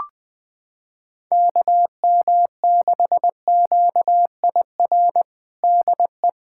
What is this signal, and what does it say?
Morse code, heard as a keyed tone, about 20 wpm, 700 Hz. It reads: KM6QIR DE
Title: USA Callsign DE Callsign 20wpm